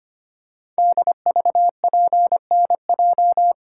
Morse code, heard as a keyed tone, about 25 wpm, 700 Hz. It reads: DVPNJ
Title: Code Groups - 5 Letters - 25wpm